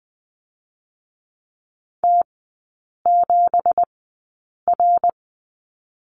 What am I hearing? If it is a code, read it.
Morse code, heard as a keyed tone, about 20 wpm, spaced as for 8 wpm, 700 Hz. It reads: T7R
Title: 166 Lesson 27e - Number 7 - Three Characters 20wpm